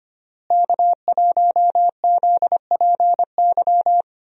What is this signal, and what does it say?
Morse code, heard as a keyed tone, about 25 wpm, 700 Hz. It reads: K1ZPY
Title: USA Call Signs - 5 Characters - Repeated 3x - 25wpm